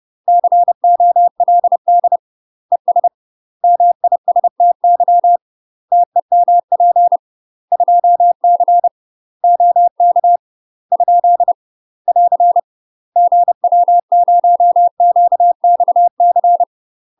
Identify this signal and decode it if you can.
Morse code, heard as a keyed tone, about 30 wpm, 700 Hz. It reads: COLD ES MISTY TEMP 2C OK ? <AR> GW0QXC
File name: UK MASTER QSOs TRAINING / 30WPM / UK MASTER QSOs 0040